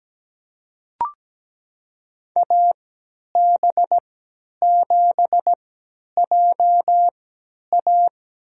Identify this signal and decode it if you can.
Morse code, heard as a keyed tone, about 17 wpm, spaced as for 10 wpm, 700 Hz. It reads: AB7JA
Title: USA Call Signs - 3x Character Spacing - 17wpm